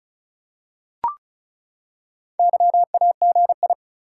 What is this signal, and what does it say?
Morse code, heard as a keyed tone, about 36 wpm, 700 Hz. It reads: YAGI
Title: Top 180 QSO Elements 35wpm